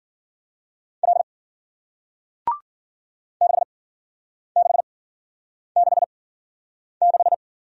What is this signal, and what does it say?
Morse code, heard as a keyed tone, about 51 wpm, spaced as for 9 wpm, 700 Hz. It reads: L 6666